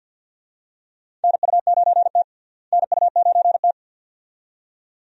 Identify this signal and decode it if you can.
Morse code, heard as a keyed tone, about 47 wpm, 700 Hz. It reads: NU9T NU9T
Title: USA Call Signs - 4 Characters - Repeated 2x - 50wpm